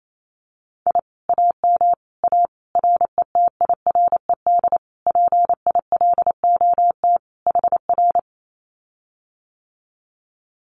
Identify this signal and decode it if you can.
Morse code, heard as a keyed tone, about 28 wpm, 700 Hz. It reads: I AM A RETIRED PILOT HR
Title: Common QSO Phrases Encore 28wpm